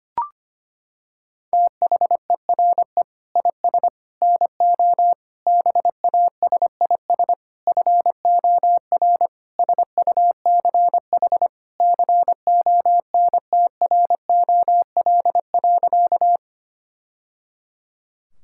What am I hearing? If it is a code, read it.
Morse code, heard as a keyed tone, about 25 wpm, 700 Hz. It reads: THERE IS NO BASIS FOR SUCH CONTROL.
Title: Sentences from Top 1000 Words - Review Part 5 - 25wpm